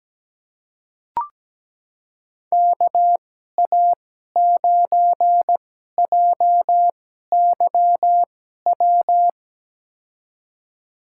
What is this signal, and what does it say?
Morse code, heard as a keyed tone, about 17 wpm, spaced as for 12 wpm, 700 Hz. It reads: KA9JYW
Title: USA Call Signs - 6 Characters - 2x Character Spacing - 17wpm